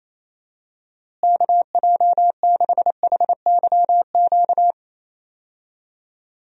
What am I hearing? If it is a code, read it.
Morse code, heard as a keyed tone, about 28 wpm, 700 Hz. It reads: KJ6HYQ
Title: USA Call Signs - 6 Characters - Repeated 2x - 28wpm